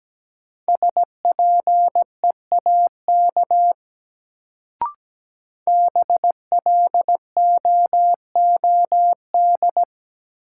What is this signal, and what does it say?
Morse code, heard as a keyed tone, about 17 wpm, 700 Hz. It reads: SPEAK BLOOD
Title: Sentences from Top 1000 Words - Review Part 2 - New Word Drill - 17wpm